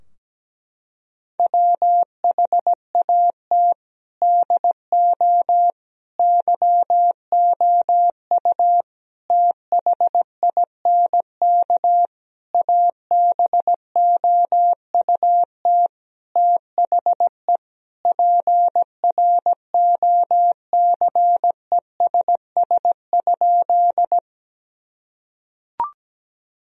Morse code, heard as a keyed tone, about 17 wpm, 700 Hz. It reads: WHAT DO YOU THINK ABOUT THE PROCESS?
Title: Sentences from Top 1000 Words - Review Part 3 - 17wpm